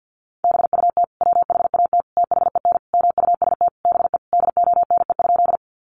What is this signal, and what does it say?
Morse code, heard as a keyed tone, about 50 wpm, 700 Hz. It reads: BUT WHAT THEN MUST BE DONE?
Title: Sentences from Top 500 Words 50wpm